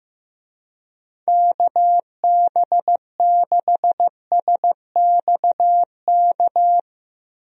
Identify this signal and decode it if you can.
Morse code, heard as a keyed tone, about 15 wpm, 700 Hz. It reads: KB6SXK
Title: USA Call Signs - 6 Characters - Repeated 2x - 15wpm